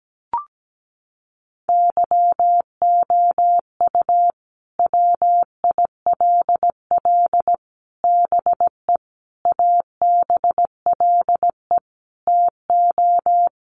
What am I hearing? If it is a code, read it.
Morse code, heard as a keyed tone, about 17 wpm, 700 Hz. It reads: YOU WILL BE ABLE TO
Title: Sets of 5 Words Encore2 17wpm